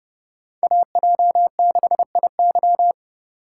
Morse code, heard as a keyed tone, about 30 wpm, 700 Hz. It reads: AJ6IY
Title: USA Call Signs - 5 Characters - 3x Character Spacing - 30wpm